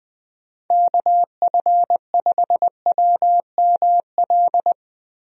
Morse code, heard as a keyed tone, about 20 wpm, 700 Hz. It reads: KF5WML
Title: USA Call Signs - 6 Characters - Repeated 3x - 20wpm